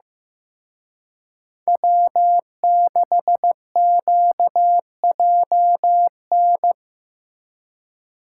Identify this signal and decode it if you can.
Morse code, heard as a keyed tone, about 16 wpm, 700 Hz. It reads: W6QJN
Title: Speed Racing - USA 1x3 Call Signs 15wpm